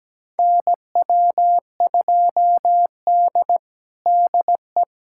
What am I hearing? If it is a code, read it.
Morse code, heard as a keyed tone, about 17 wpm, 700 Hz. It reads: NW2D DE
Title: USA Callsign DE Callsign - 4 Characters - 17wpm